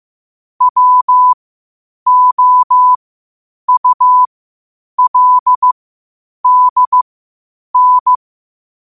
Morse code, heard as a keyed tone, about 15 wpm, spaced as for 8 wpm, 1000 Hz. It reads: WOULDN
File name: qotdfiles / 8wpm